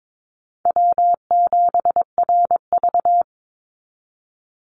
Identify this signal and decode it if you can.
Morse code, heard as a keyed tone, about 22 wpm, 700 Hz. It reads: W7RV
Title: NA SOTA Chaser Callsigns 22wpm